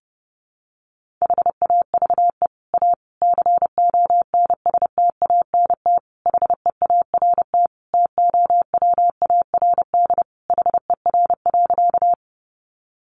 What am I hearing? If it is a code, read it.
Morse code, heard as a keyed tone, about 28 wpm, 700 Hz. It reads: HAVE A CONSTANT HEART TOWARD HER.
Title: Sentences from Top 1500 Words 30wpm